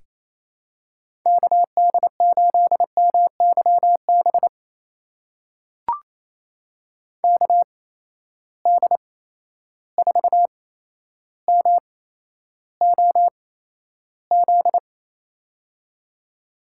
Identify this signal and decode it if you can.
Morse code, heard as a keyed tone, about 27 wpm, spaced as for 12 wpm, 700 Hz. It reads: KD8MYB KD4MOZ
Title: USA Call Signs - 6 Characters - 8x Character Spacing - 28wpm